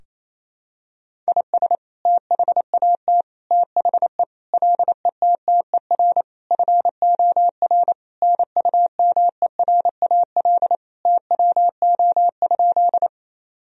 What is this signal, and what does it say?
Morse code, heard as a keyed tone, about 28 wpm, 700 Hz. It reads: IS THAT THE LETTER FOR NUMERAL TWO?
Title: Sentences from Top 1000 Words - Review Part 2 - 28wpm